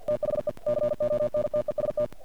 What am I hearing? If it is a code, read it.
Morse code, heard as a keyed tone, about 48 wpm, 590 Hz. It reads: THE CONTEST
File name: archived / CRSnet